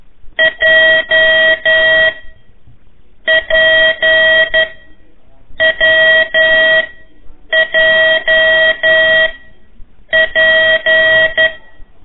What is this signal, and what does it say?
Morse code, heard as a keyed tone, about 10 wpm, spaced as for 6 wpm, 1900 Hz. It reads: JPWJP
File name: 3 lezione CW 1 AIR